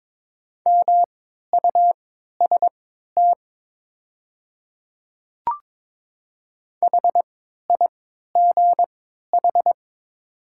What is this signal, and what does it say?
Morse code, heard as a keyed tone, about 22 wpm, spaced as for 13 wpm, 690 Hz. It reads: MUST HIGH
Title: 3x Character Spacing - Top 200 Words - 22wpm